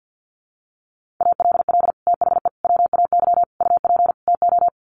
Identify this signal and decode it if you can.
Morse code, heard as a keyed tone, about 50 wpm, 700 Hz. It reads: ALL THE WAY UP TO
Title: Sets of 5 Words Encore 50wpm